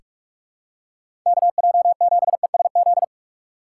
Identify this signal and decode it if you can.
Morse code, heard as a keyed tone, about 47 wpm, 700 Hz. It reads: KJ7ESZ
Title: USA Call Signs - 6 Characters - 6x Character Spacing - 45wpm